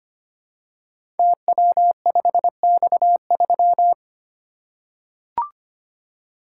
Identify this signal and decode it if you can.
Morse code, heard as a keyed tone, about 25 wpm, 700 Hz. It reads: TW5X3
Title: Code Groups - 5 Letter:Numbers - 25wpm